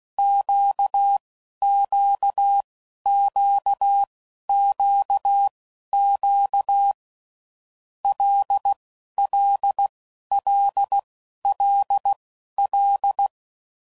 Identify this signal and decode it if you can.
Morse code, heard as a keyed tone, about 16 wpm, spaced as for 11 wpm, 800 Hz. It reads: QQQQQ LLLLL
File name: Groups / Basic Letter Groups / LFCYQ